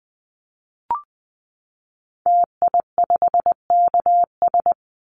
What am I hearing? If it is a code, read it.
Morse code, heard as a keyed tone, about 19 wpm, 700 Hz. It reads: TI5KS
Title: Call Signs Repeated 2x - 20wpm